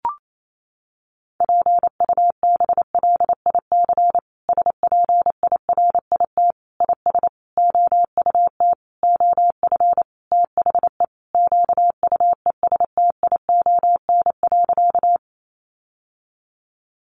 Morse code, heard as a keyed tone, about 28 wpm, 700 Hz. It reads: PUBLIC SPIRIT IS OUT OF THE QUESTION.
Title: Sentences from Top 1400 Words 28wpm